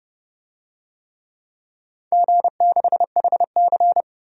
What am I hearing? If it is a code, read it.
Morse code, heard as a keyed tone, about 30 wpm, 700 Hz. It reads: G6HC
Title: DXCC-100 Call Signs - 4 Characters - 30wpm